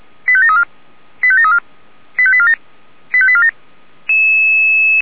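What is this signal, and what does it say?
Five-tone selective call (ZVEI), heard as quick runs of five tones, tones of about 70 ms. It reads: 85734 85734 85847 85847
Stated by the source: ZVEI five-tone selective call received on SDR.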